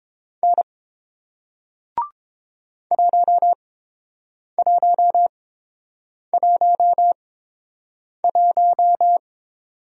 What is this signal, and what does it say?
Morse code, heard as a keyed tone, about 30 wpm, spaced as for 7 wpm, 700 Hz. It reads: N 1111